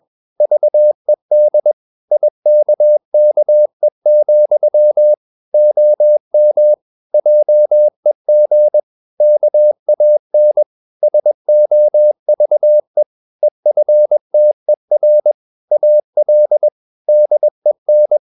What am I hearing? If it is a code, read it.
Morse code, heard as a keyed tone, about 21 wpm, 600 Hz. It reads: VED IKKE, OM JEG KAN SOVE EFTER AL DEN
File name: Ferie QRS 21wpm